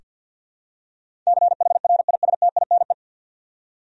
Morse code, heard as a keyed tone, about 47 wpm, 700 Hz. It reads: CHRISTINE
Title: USA Common Names - 50wpm